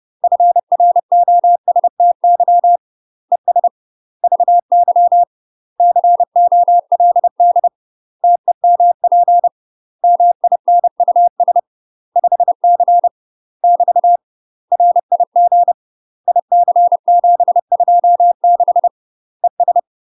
Morse code, heard as a keyed tone, about 30 wpm, 700 Hz. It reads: FROSTY ES VY COLD TEMP MINUS 5C <BT> RIG IC726 ES
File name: UK MASTER QSOs TRAINING / 30WPM / UK MASTER QSOs 0040